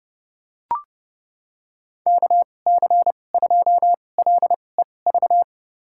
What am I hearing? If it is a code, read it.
Morse code, heard as a keyed tone, about 30 wpm, spaced as for 22 wpm, 700 Hz. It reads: KC2LEV
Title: USA Call Signs - 6 Characters - 2x Character Spacing - 30wpm